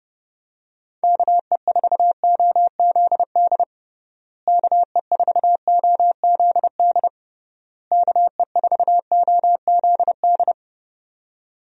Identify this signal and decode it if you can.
Morse code, heard as a keyed tone, about 30 wpm, 700 Hz. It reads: KE4OZD KE4OZD KE4OZD
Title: USA Call Signs - 6 Characters - Repeated 3x - 30wpm